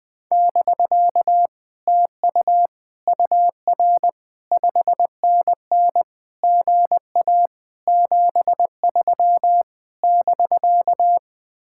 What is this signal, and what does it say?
Morse code, heard as a keyed tone, about 19 wpm, 700 Hz. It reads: <BK> TU UR 5NN GA 73 <BK>
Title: Speed-Racing Common QSO Phrases - 20wpm